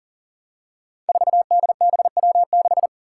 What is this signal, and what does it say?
Morse code, heard as a keyed tone, about 42 wpm, 690 Hz. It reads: 4DBW6
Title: Five Challenging Letter-Number Combinations 40wpm